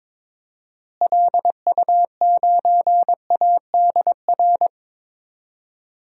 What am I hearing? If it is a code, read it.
Morse code, heard as a keyed tone, about 22 wpm, 700 Hz. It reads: LU9ADR
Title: DXCC-100 Call Signs - 6 Characters - Repeated 2x - 22wpm